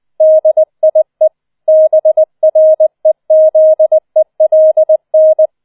Morse code, heard as a keyed tone, about 20 wpm, 630 Hz. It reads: DIE BREZELN